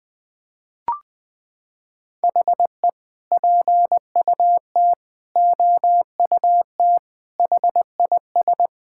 Morse code, heard as a keyed tone, about 20 wpm, 700 Hz. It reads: HE PUT OUT HIS
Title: Sentences from Top 1000 Words - Review Part 5 - 20wpm